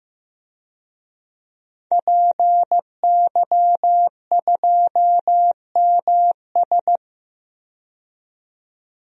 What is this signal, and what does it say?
Morse code, heard as a keyed tone, about 15 wpm, 700 Hz. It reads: PY2MS
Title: DXCC-100 Call Signs - 5 Characters - 15wpm